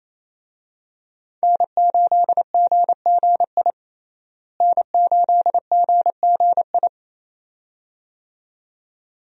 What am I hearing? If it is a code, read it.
Morse code, heard as a keyed tone, about 28 wpm, 700 Hz. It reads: N8GGI N8GGI
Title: USA Call Signs - 5 Characters - Repeated 2x - 28wpm